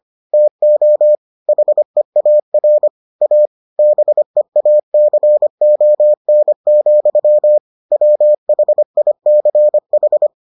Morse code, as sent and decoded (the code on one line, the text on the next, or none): - --- / .... .- .-. / .- / -... .- -.-. --- -. --..-- / .-- .... .. -.-. ....
TO HEAR A BEACON, WHICH